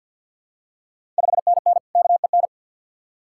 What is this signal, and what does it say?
Morse code, heard as a keyed tone, about 51 wpm, 700 Hz. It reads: 5NN KEN
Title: Ontario QSO Party Exchange 50wpm